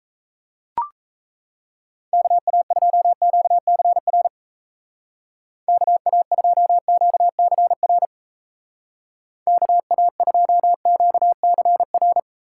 Morse code, heard as a keyed tone, about 36 wpm, 700 Hz. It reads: KA2QCR KA2QCR KA2QCR